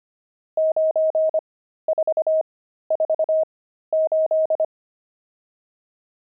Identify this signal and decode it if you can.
Morse code, heard as a keyed tone, about 25 wpm, spaced as for 13 wpm, 630 Hz. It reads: 9448
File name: pangram-fs-13-cs-25-nums3